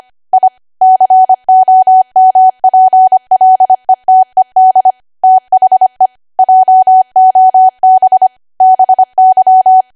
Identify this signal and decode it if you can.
Morse code, heard as a keyed tone, about 25 wpm, 750 Hz. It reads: I COMPLETED THE JOB BY
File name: Archive / 25 WPM / 25WPM